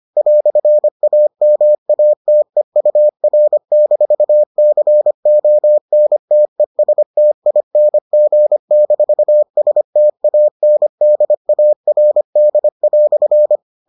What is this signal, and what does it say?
Morse code, heard as a keyed tone, about 25 wpm, 600 Hz. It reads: "AMATEUR-CONTESTING-STANDARD"